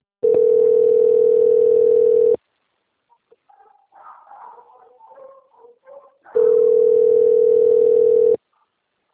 Ringback tone (North America), 440 + 480 Hz.